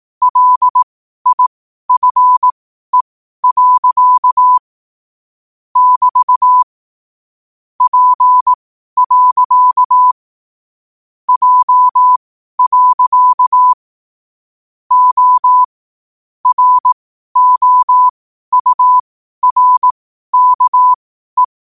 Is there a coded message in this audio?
Yes: Morse code from 0.2 to 21.4 s, heard as a keyed tone, about 18 wpm, spaced as for 12 wpm, 1000 Hz. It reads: LIFE. <BT> P. J. O ROURKE